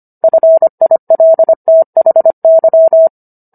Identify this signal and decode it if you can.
Morse code, heard as a keyed tone, about 25 wpm, 650 Hz. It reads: FILTHY